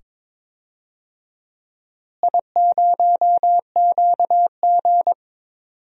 Morse code, heard as a keyed tone, about 21 wpm, 700 Hz. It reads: I0QG
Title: DXCC-100 Call Signs - 4 Characters - Repeated 2x - 22wpm